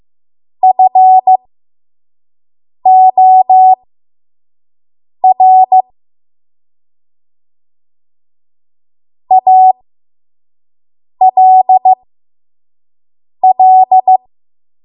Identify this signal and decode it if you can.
Morse code, heard as a keyed tone, about 15 wpm, spaced as for 5 wpm, 750 Hz. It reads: FOR ALL
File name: Morse / 05WPM